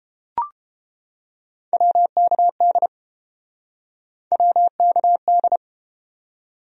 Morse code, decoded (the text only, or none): WKD WKD